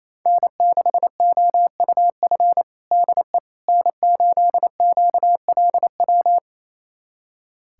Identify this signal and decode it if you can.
Morse code, heard as a keyed tone, about 28 wpm, 700 Hz. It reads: N6OUF DE N8QLW
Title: USA Callsign DE Callsign - 5 Characters - 28wpm